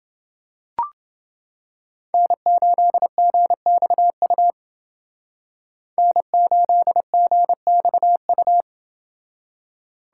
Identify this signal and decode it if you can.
Morse code, heard as a keyed tone, about 29 wpm, 700 Hz. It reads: N8GXU N8GXU